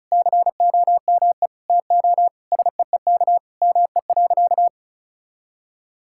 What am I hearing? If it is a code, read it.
Morse code, heard as a keyed tone, about 36 wpm, 700 Hz. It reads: COME TO SEEK ME.
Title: Sentences from Top 1600 Words 35wpm